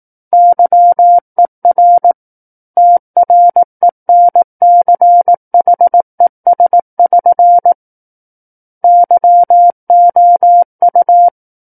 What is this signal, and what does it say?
Morse code, heard as a keyed tone, about 18 wpm, 700 Hz. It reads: YER TRENCHES<SN> YOU
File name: 18 WPM